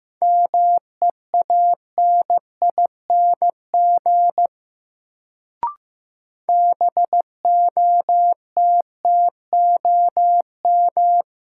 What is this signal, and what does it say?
Morse code, heard as a keyed tone, about 15 wpm, 700 Hz. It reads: MEANING BOTTOM
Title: Sentences from Top 900 Words - New Word Drill - 15wpm